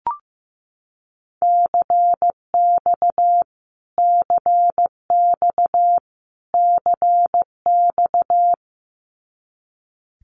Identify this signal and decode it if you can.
Morse code, heard as a keyed tone, about 15 wpm, 700 Hz. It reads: CX CX CX
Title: Abbreviations Repeated 3x - 15wpm